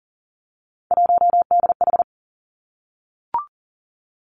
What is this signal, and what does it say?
Morse code, heard as a keyed tone, about 40 wpm, 700 Hz. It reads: 1DH
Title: Three Challenging Letter-Number Combinations 40wpm